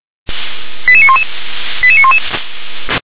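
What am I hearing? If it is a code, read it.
Five-tone selective call (ZVEI), heard as quick runs of five tones, tones of about 70 ms. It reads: 88911 88911
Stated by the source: ZVEI five-tone selective call received on SDR.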